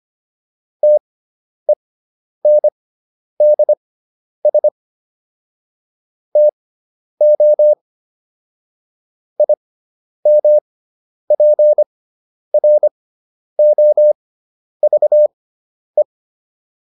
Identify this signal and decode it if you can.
Morse code, heard as a keyed tone, about 25 wpm, spaced as for 10 wpm, 600 Hz. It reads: TENDS TO IMPROVE